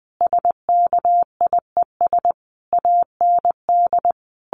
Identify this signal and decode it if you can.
Morse code, heard as a keyed tone, about 20 wpm, 700 Hz. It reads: SKIES AND
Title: A Sound Of Thunder 20wpm